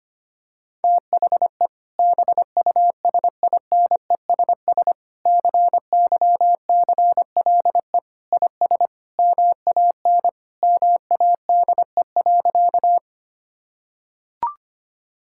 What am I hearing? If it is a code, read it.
Morse code, heard as a keyed tone, about 25 wpm, 700 Hz. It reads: THE BUSINESS CYCLE IS MAN MADE.
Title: Sentences from Top 1000 Words - Review Part 6 - 25wpm